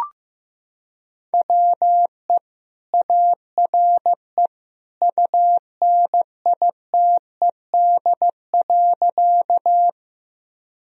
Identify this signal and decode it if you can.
Morse code, heard as a keyed tone, about 15 wpm, 700 Hz. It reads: WE ARE UNITED.
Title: Sentences from Top 1000 Words - Review Part 4 - 15wpm